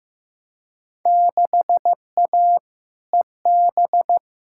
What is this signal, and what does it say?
Morse code, heard as a keyed tone, about 15 wpm, 700 Hz. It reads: 6A EB
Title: Field Day Exchanges Encore - 15wpm